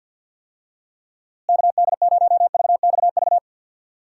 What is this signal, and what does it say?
Morse code, heard as a keyed tone, about 52 wpm, 700 Hz. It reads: KD0VXV
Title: USA Call Signs - 6 Characters - Repeated 3x - 50wpm